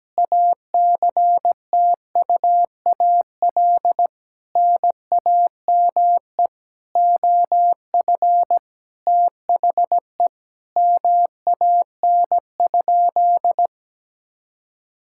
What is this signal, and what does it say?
Morse code, heard as a keyed tone, about 17 wpm, 700 Hz. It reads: ACTUAL NAME OF THE MAN?
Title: Sentences from Top 1000 Words - Review Part 5 - 17wpm